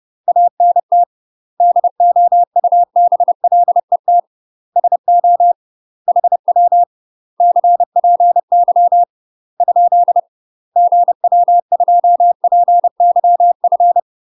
Morse code, heard as a keyed tone, about 30 wpm, 700 Hz. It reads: ANT DOUBLET SO HW CPY ? GW2PYF
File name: UK MASTER QSOs TRAINING / 30WPM / UK MASTER QSOs 0026